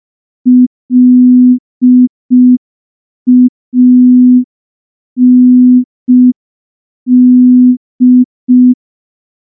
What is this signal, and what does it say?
Morse code, heard as a keyed tone, about 5 wpm, 250 Hz. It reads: LAND
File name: morse / morse-training-05WPM